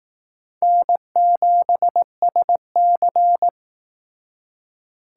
Morse code, heard as a keyed tone, about 19 wpm, 700 Hz. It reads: N7SC
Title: Speed Racing - USA 1x2 Call Signs 15wpm